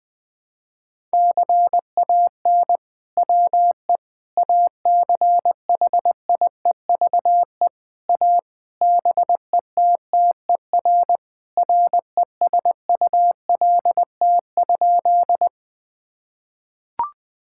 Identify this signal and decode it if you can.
Morse code, heard as a keyed tone, about 20 wpm, 700 Hz. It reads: CAN WE ACHIEVE A BETTER RESULT?
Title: Sentences from Top 1000 Words - Review Part 6 - 20wpm